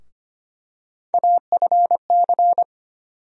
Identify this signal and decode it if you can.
Morse code, heard as a keyed tone, about 25 wpm, 700 Hz. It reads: AFC
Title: USA Call Sign Postfixes - 3 Letters 25wpm